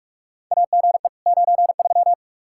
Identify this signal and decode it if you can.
Morse code, heard as a keyed tone, about 45 wpm, 700 Hz. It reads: AGE 93
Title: Practice QSO - Exchange 3 - More Info - 45wpm